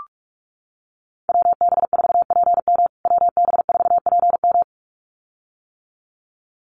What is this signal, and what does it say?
Morse code, heard as a keyed tone, about 44 wpm, 700 Hz. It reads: WB4PM WB4PM
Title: USA Call Signs Repeated 2x 45wpm